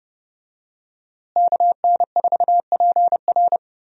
Morse code, heard as a keyed tone, about 30 wpm, 700 Hz. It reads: KN4PR
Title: USA Call Signs - 5 Characters - Repeated 2x - 30wpm